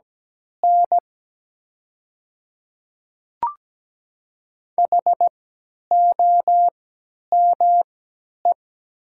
Morse code, heard as a keyed tone, about 17 wpm, spaced as for 10 wpm, 700 Hz. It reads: N HOME
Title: 3x Character Spacing - Top 300 Words - 17wpm